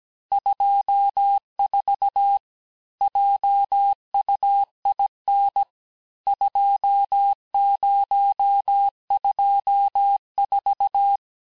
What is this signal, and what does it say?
Morse code, heard as a keyed tone, about 17 wpm, 800 Hz. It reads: 24 JUIN 2024
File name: cours-du-lundi files / cours-1587-texte-chiffre-vitesse-16MPM